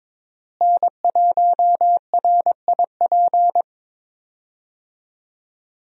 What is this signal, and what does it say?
Morse code, heard as a keyed tone, about 22 wpm, 700 Hz. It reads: N1RIP
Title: USA Call Signs - 5 Characters - Repeated 3x - 22wpm